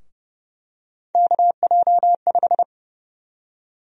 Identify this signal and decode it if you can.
Morse code, heard as a keyed tone, about 29 wpm, 700 Hz. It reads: KJ5
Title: USA Call Sign Prefixes - 30wpm